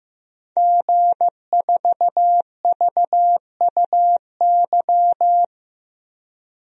Morse code, heard as a keyed tone, about 15 wpm, 700 Hz. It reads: G4VUY
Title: DXCC-100 Call Signs - 5 Characters - 15wpm